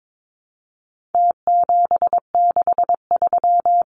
Morse code, heard as a keyed tone, about 22 wpm, 700 Hz. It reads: T763
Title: Numbers - Contest Style - 4-Digit 22wpm